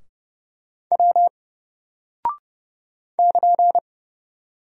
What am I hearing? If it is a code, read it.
Morse code, heard as a keyed tone, about 30 wpm, 700 Hz. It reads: W <KN>
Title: Single Character (Letter:Number:Punctuation:Prosign) 30wpm